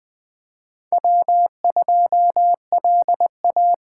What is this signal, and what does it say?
Morse code, heard as a keyed tone, about 20 wpm, 700 Hz. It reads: W2LA
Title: Speed Racing - USA 1x2 Call Signs 17wpm